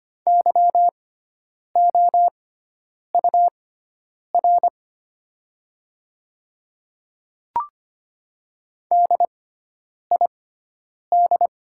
Morse code, heard as a keyed tone, about 25 wpm, spaced as for 9 wpm, 700 Hz. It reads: YOUR DID
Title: 6x Character Spacing - Top 100 Words - 25wpm